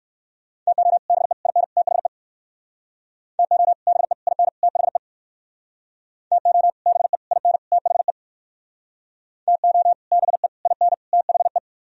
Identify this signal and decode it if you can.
Morse code, heard as a keyed tone, about 60 wpm, 700 Hz. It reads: TO BE IN THE TO BE IN THE TO BE IN THE TO BE IN THE